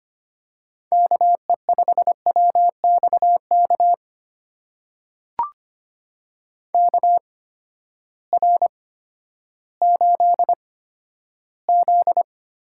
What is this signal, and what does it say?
Morse code, heard as a keyed tone, about 25 wpm, spaced as for 12 wpm, 700 Hz. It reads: KE5WXK KR8Z
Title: USA Call Signs - 6 Characters - 8x Character Spacing - 25wpm